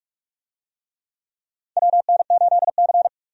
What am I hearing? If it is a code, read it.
Morse code, heard as a keyed tone, about 43 wpm, 700 Hz. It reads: WN8C
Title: USA Call Signs - 4 Characters - 45wpm